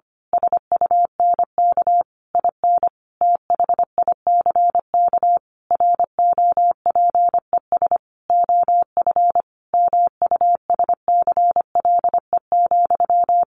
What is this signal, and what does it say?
Morse code, heard as a keyed tone, about 25 wpm, 700 Hz. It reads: SUNK IN THICK ROPES OF MUSCLE,
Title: A Sound Of Thunder 25wpm